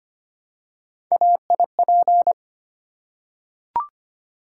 Morse code, heard as a keyed tone, about 25 wpm, 700 Hz. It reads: AIP
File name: Three Letter-Number Combinations - 25wpm